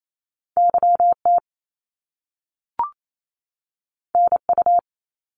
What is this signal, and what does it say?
Morse code, heard as a keyed tone, about 28 wpm, 700 Hz. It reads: YT NU
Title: Canadian Provinces and Territories 28wpm